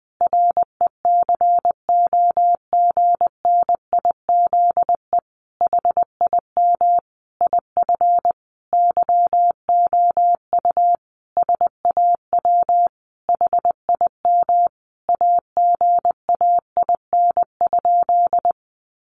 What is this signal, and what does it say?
Morse code, heard as a keyed tone, about 20 wpm, 700 Hz. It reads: RECOGNIZE HIM IF YOU SAW HIM AGAIN?
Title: Sentences from Top 1400 Words 20wpm